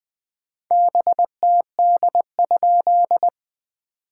Morse code, heard as a keyed tone, about 20 wpm, 700 Hz. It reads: BTD?
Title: POTA QSO Practice - 20wpm